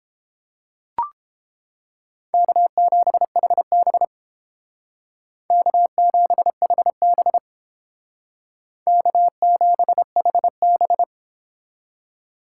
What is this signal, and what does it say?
Morse code, heard as a keyed tone, about 31 wpm, 700 Hz. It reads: K7HB K7HB K7HB